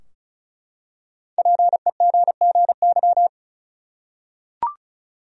Morse code, heard as a keyed tone, about 36 wpm, 700 Hz. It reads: PEGGY
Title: USA Common Names - Max 5 Letters - 35wpm